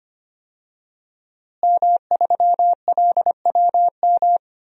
Morse code, heard as a keyed tone, about 25 wpm, 700 Hz. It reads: M3LWM
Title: DXCC-100 Call Signs - 5 Characters - Repeated 2x - 25wpm